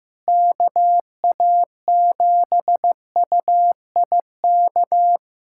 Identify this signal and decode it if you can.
Morse code, heard as a keyed tone, about 15 wpm, 700 Hz. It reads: KA7UIK
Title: USA Callsign DE Callsign - 6 Characters - 15wpm